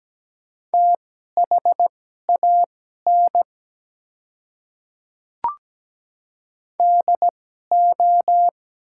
Morse code, heard as a keyed tone, about 17 wpm, spaced as for 12 wpm, 700 Hz. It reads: THAN DO
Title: 2x Character Spacing - Top 100 Words - 17wpm